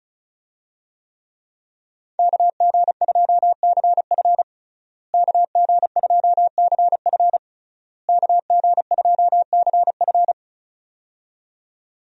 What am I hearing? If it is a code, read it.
Morse code, heard as a keyed tone, about 36 wpm, 700 Hz. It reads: KG2CF KG2CF KG2CF
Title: USA Call Signs - 5 Characters - Repeated 3x - 35wpm